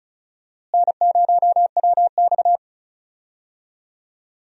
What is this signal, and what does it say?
Morse code, heard as a keyed tone, about 36 wpm, 700 Hz. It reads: N0WX
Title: USA Call Signs - 4 Characters - Repeated 3x - 35wpm